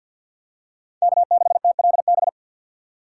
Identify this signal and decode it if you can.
Morse code, heard as a keyed tone, about 50 wpm, 690 Hz. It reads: K6TLB
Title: USA Call Signs - 5 Characters - 4x Character Spacing - 50wpm